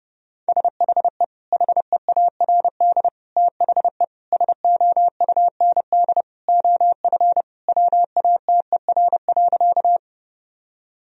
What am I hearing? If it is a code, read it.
Morse code, heard as a keyed tone, about 30 wpm, 700 Hz. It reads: SHE HEARD THE SOUND OF WATER.
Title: Sentences from Top 1000 Words - Review Part 1 - 30wpm